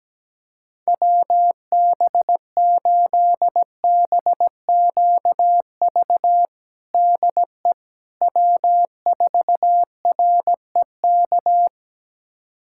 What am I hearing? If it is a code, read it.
Morse code, heard as a keyed tone, about 17 wpm, 700 Hz. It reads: WB8BQV DE W4REK
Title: USA Callsign DE Callsign 17wpm